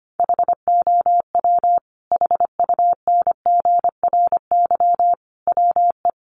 Morse code, heard as a keyed tone, about 24 wpm, 700 Hz. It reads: HOW HUNGRY WE
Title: Sentences from Top 1400 Words 25wpm